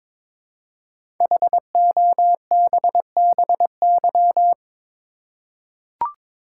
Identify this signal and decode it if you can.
Morse code, heard as a keyed tone, about 22 wpm, 700 Hz. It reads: HOBBY
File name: 5 Letter Words - First 3 Letters Given - 22wpm